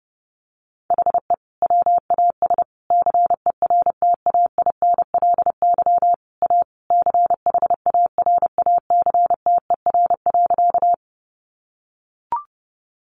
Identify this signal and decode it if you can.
Morse code, heard as a keyed tone, about 30 wpm, 700 Hz. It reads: HE WAS CERTAINLY A CHARACTER.
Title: Sentences from Top 1200 Words 30wpm